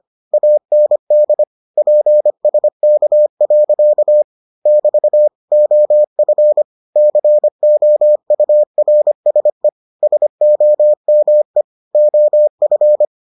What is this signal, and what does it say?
Morse code, heard as a keyed tone, about 25 wpm, 600 Hz. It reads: AND PSK. <BT> OF COURSE SOME OF